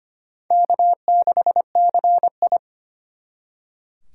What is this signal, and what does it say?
Morse code, heard as a keyed tone, about 31 wpm, 700 Hz. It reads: K6CI
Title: Speed Racing - USA 1x2 Call Signs 25wpm